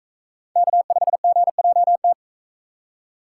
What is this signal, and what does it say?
Morse code, heard as a keyed tone, about 44 wpm, 700 Hz. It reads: K5GJT